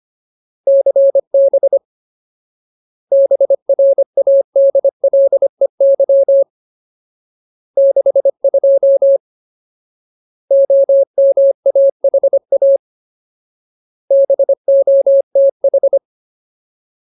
Morse code, heard as a keyed tone, about 25 wpm, 550 Hz. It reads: CB BRADLEY 62 OMAHA BOTH
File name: Ultimate Word List / 25WPM Ultimate